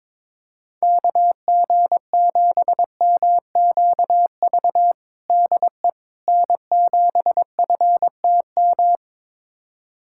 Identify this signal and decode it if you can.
Morse code, heard as a keyed tone, about 21 wpm, 700 Hz. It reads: KG7MQV DE N7FTM
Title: USA Callsign DE Callsign 22wpm